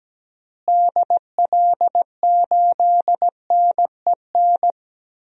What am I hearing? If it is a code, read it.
Morse code, heard as a keyed tone, about 17 wpm, 700 Hz. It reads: DL8NEN
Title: DXCC-100 Call Signs - 6 Characters - Repeated 3x - 17wpm